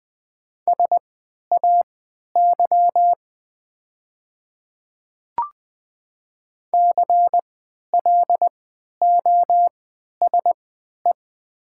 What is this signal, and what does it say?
Morse code, heard as a keyed tone, about 20 wpm, spaced as for 11 wpm, 700 Hz. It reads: SAY CLOSE
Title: 3x Character Spacing - Top 300 Words - 20wpm